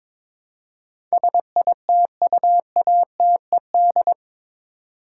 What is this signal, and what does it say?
Morse code, heard as a keyed tone, about 21 wpm, 700 Hz. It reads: SITUATED
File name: practice / 22wpm / Eight Letter Words - 22wpm